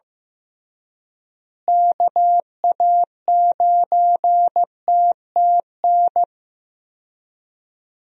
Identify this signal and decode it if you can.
Morse code, heard as a keyed tone, about 15 wpm, 700 Hz. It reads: KA9TTN
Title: USA Call Signs - 6 Characters - Repeated 2x - 15wpm